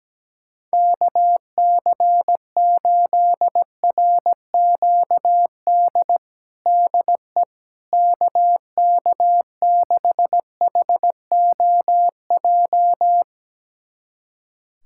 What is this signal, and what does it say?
Morse code, heard as a keyed tone, about 17 wpm, 700 Hz. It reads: KC8RQD DE KK6HOJ
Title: USA Callsign DE Callsign - 6 Characters - 17wpm